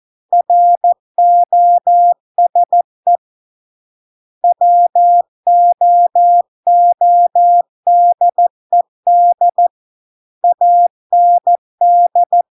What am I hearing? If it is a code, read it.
Morse code, heard as a keyed tone, about 14 wpm, 700 Hz. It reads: ROSE WOODED AND